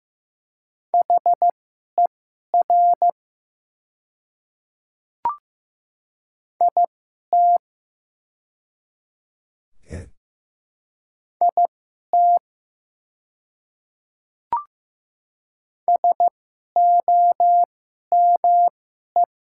Morse code, heard as a keyed tone, about 15 wpm, spaced as for 11 wpm, 690 Hz. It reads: HER IT IT SOME
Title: 2x Character Spacing - Top 100 Words - 15wpm